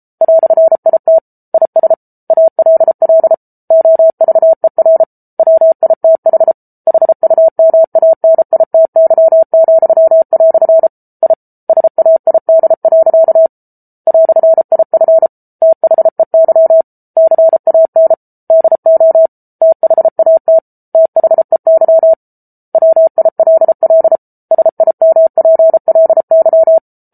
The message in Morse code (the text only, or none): "IT IS ALL OVER WITH HUMANITY," I SAID. "IF THEY CAN DO THAT THEY WILL SIMPLY